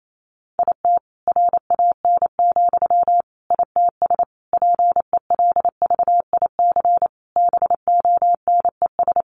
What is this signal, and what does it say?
Morse code, heard as a keyed tone, about 28 wpm, 700 Hz. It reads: IT RAN, ITS PELVIC BONES
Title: A Sound Of Thunder 28wpm